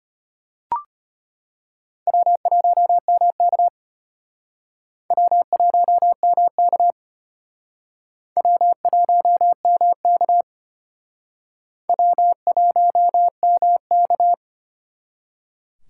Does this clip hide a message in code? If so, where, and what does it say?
Yes: Morse code from 2.1 to 14.3 s, heard as a keyed tone, about 32 wpm, 700 Hz. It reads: W1MK W1MK W1MK W1MK